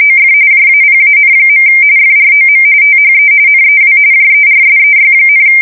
RTTY, about 45.45 baud. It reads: OF JAPAN. FROM THE NEW JERSEY,
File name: fsk170 / battleshipnj